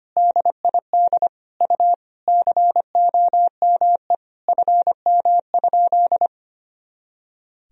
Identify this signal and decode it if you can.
Morse code, heard as a keyed tone, about 25 wpm, 700 Hz. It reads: DID U COME FM?
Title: Sentences from Top 200 Words and Abbreviations 25wpm